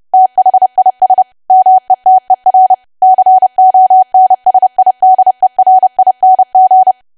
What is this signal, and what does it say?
Morse code, heard as a keyed tone, about 30 wpm, 750 Hz. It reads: THIS METER CONSIDERING
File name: cwa / W1AW-Level3 / 30WPM